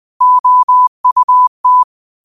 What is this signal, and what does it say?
Morse code, heard as a keyed tone, about 20 wpm, 1000 Hz. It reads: OUT
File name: qotdfiles / 20wpm